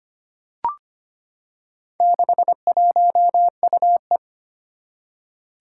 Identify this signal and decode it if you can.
Morse code, heard as a keyed tone, about 25 wpm, 700 Hz. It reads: 61UE
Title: Four Challenging Letter-Number Combinations 25wpm